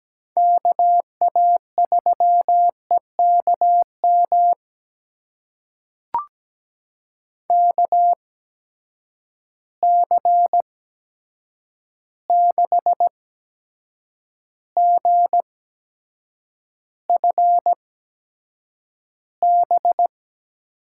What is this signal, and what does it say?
Morse code, heard as a keyed tone, about 17 wpm, spaced as for 7 wpm, 700 Hz. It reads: KA3EKM KC6GFB
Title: USA Call Signs - 6 Characters - 8x Character Spacing - 17wpm